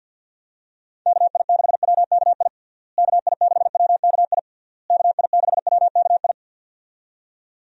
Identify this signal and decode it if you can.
Morse code, heard as a keyed tone, about 49 wpm, 700 Hz. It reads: KI6WKI KI6WKI KI6WKI
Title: USA Call Signs - 6 Characters - Repeated 3x - 50wpm